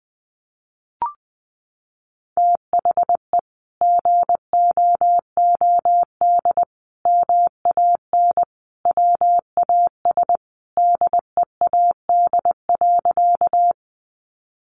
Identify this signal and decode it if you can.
Morse code, heard as a keyed tone, about 20 wpm, 700 Hz. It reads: THE GOOD MAN WAS DEAD.
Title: Sentences from Top 1500 Words 20wpm